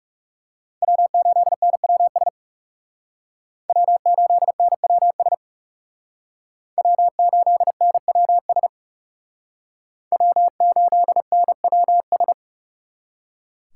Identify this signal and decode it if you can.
Morse code, heard as a keyed tone, about 36 wpm, 700 Hz. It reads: W8NWS W8NWS W8NWS W8NWS
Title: Speed Racing - USA 1x3 Call Signs 30wpm